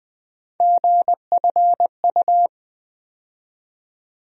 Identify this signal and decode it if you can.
Morse code, heard as a keyed tone, about 20 wpm, 700 Hz. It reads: GFU
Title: USA Call Sign Postfixes - 3 Letters 20wpm